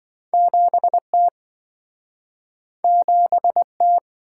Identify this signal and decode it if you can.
Morse code, heard as a keyed tone, about 27 wpm, 700 Hz. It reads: 7T 7T